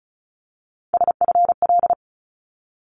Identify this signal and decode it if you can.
Morse code, heard as a keyed tone, about 36 wpm, 700 Hz. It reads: SFL
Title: ARRL Sections 35wpm